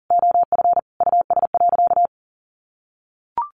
Morse code, heard as a keyed tone, about 37 wpm, 700 Hz. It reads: OF US.
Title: Sentences from Top 1100 Words 40wpm